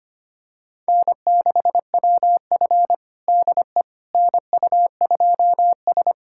Morse code, heard as a keyed tone, about 25 wpm, 700 Hz. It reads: N6WF DE NU2S
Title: USA Callsign DE Callsign - 4 Characters - 25wpm